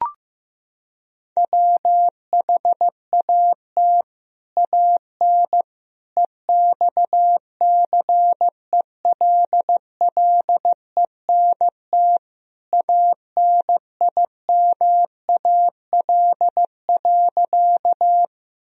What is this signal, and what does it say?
Morse code, heard as a keyed tone, about 15 wpm, 700 Hz. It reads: WHAT AN EXCELLENT ANIMAL.
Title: Sentences from Top 1000 Words - Review Part 1 - 15wpm